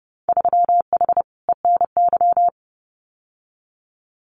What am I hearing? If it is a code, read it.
Morse code, heard as a keyed tone, about 30 wpm, 700 Hz. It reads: 3H ENY
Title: Winter Field Day Exchanges Encore 30wpm